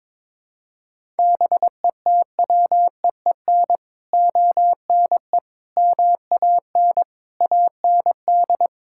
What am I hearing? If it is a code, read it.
Morse code, heard as a keyed tone, about 22 wpm, 700 Hz. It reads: BETWEEN ONE MAN AND
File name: practice / 22wpm / Sets of 6 Words - 22wpm